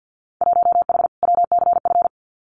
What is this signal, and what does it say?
Morse code, heard as a keyed tone, about 48 wpm, 700 Hz. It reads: 1H WCF
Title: Winter Field Day Exchanges 50wpm